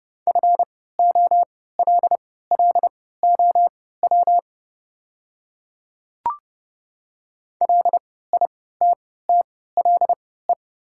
Morse code, heard as a keyed tone, about 30 wpm, spaced as for 17 wpm, 700 Hz. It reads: FOLLOW LITTLE
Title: 3x Character Spacing - Top 200 Words - 30wpm